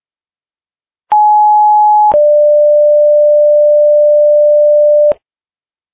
Two-tone page, heard as a one-second tone, then a three-second tone, about 840 then 590 Hz. It is heard from 1.1 to 5.1 s.